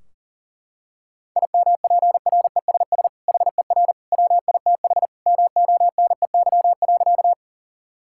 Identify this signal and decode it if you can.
Morse code, heard as a keyed tone, about 40 wpm, 700 Hz. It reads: IMPRESS HER WITH MONEY.
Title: Sentences from Top 1600 Words 40wpm